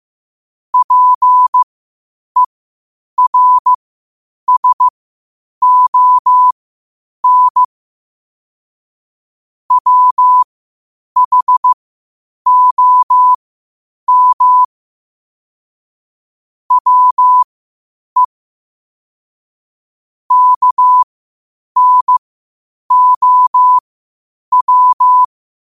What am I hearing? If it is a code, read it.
Morse code, heard as a keyed tone, about 15 wpm, spaced as for 8 wpm, 1000 Hz. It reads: PERSON WHOM WE KNOW
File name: qotdfiles / 8wpm